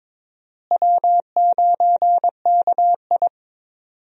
Morse code, heard as a keyed tone, about 22 wpm, 700 Hz. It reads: W9KI
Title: USA Call Signs - 4 Characters - Repeated 3x - 22wpm